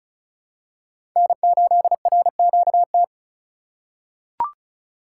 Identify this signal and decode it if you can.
Morse code, heard as a keyed tone, about 34 wpm, 700 Hz. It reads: N8RQT
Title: USA Call Signs - 5 Characters - 8x Character Spacing - 35wpm